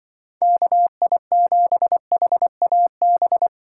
Morse code, heard as a keyed tone, about 25 wpm, 700 Hz. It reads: KI7HAB
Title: Speed Racing - USA 2x3 Call Signs 20wpm